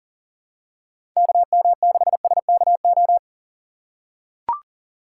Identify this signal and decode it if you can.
Morse code, heard as a keyed tone, about 40 wpm, 700 Hz. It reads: KM6SKO
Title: USA Call Signs - 6 Characters - 4x Character Spacing - 40wpm